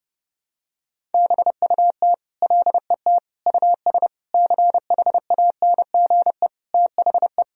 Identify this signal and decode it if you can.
Morse code, heard as a keyed tone, about 30 wpm, 700 Hz. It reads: BUT LET US CHANGE THE
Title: Sentences from Top 1000 Words - Review Part 1 - 30wpm